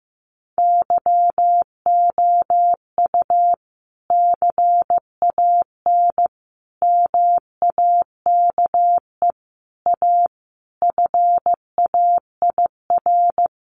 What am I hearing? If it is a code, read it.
Morse code, heard as a keyed tone, about 14 wpm, 700 Hz. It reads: YOU CAN MAKE A FAIR
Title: Sentences from Top 1400 Words 15wpm